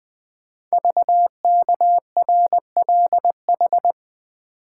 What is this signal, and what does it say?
Morse code, heard as a keyed tone, about 20 wpm, 700 Hz. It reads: VKRLH
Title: Code Groups - 5 Letters - 20wpm